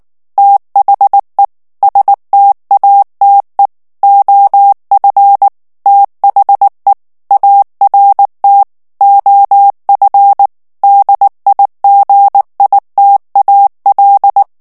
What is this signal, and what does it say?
Morse code, heard as a keyed tone, about 19 wpm, 800 Hz. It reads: THE STATE OF THE ART OF DIGITAL
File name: ANGLAIS texte clair 3 vitesse 18 mots minute